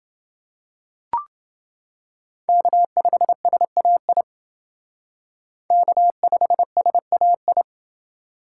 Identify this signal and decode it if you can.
Morse code, heard as a keyed tone, about 29 wpm, 700 Hz. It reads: K5SAI K5SAI